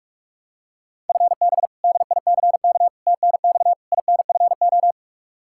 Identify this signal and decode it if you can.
Morse code, heard as a keyed tone, about 46 wpm, 700 Hz. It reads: FB DICK TNX INFO
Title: Practice QSO - Exchange 3 - More Info - 45wpm